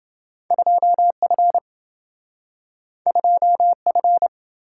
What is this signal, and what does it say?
Morse code, heard as a keyed tone, about 27 wpm, 700 Hz. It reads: 2F 2F